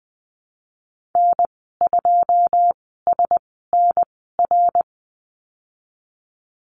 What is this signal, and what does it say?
Morse code, heard as a keyed tone, about 19 wpm, spaced as for 14 wpm, 700 Hz. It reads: N2SNR
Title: 088 Lesson 15g - Number 5 - Call Signs 20wpm